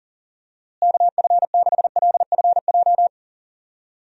Morse code, heard as a keyed tone, about 40 wpm, 700 Hz. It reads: KF6LFJ
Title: USA Call Signs - 6 Characters - Repeated 2x - 40wpm